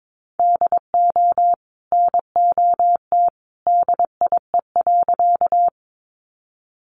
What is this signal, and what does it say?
Morse code, heard as a keyed tone, about 22 wpm, 700 Hz. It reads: DO NOT DIE.
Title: Sentences from Top 1300 Words 22wpm